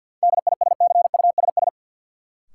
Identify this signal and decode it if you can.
Morse code, heard as a keyed tone, about 48 wpm, 700 Hz. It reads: DISCUSS
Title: Sentences from Top 1000 Words - New Word Drill - 50wpm